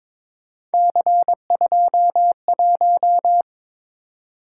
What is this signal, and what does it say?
Morse code, heard as a keyed tone, about 22 wpm, 700 Hz. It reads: C21
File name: practice / 22wpm / Three Letter-Number Combinations - 22wpm